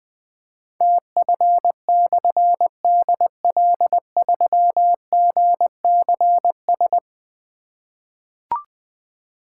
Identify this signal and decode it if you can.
Morse code, heard as a keyed tone, about 21 wpm, 700 Hz. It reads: TF/DL3GCS
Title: Difficult Call Signs 20wpm